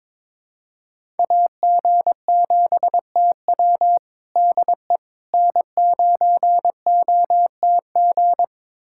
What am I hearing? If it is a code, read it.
Morse code, heard as a keyed tone, about 22 wpm, 700 Hz. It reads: AG7TW DE N9OTG
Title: USA Callsign DE Callsign - 5 Characters - 22wpm